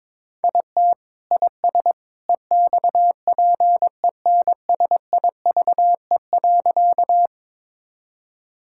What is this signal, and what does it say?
Morse code, heard as a keyed tone, about 22 wpm, 700 Hz. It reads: IT IS EXPENSIVE.
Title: Sentences from Top 1000 Words - Review Part 6 - 22wpm